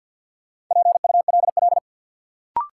Morse code, heard as a keyed tone, about 51 wpm, 700 Hz. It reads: PULL
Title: Sentences from Top 400 Words - New Word Drill - 50wpm